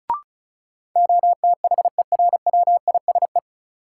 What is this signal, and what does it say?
Morse code, heard as a keyed tone, about 36 wpm, 700 Hz. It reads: OTHERWISE
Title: Sentences from Top 1200 Words - New Word Drill - 35wpm